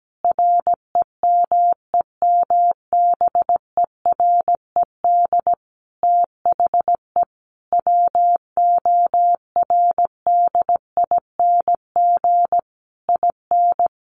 Morse code, heard as a keyed tone, about 17 wpm, 700 Hz. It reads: REMEMBERED THE WORDING IN
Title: A Sound Of Thunder 17wpm